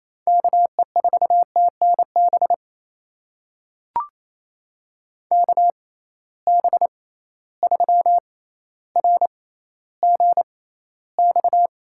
Morse code, heard as a keyed tone, about 28 wpm, spaced as for 14 wpm, 700 Hz. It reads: KE4TNB KB3RGX
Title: USA Call Signs - 6 Characters - 6x Character Spacing - 28wpm